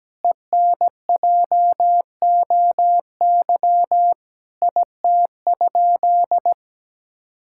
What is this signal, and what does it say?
Morse code, heard as a keyed tone, about 17 wpm, 700 Hz. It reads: ENJOY IT?
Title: Sentences from Top 1000 Words - Review Part 5 - 17wpm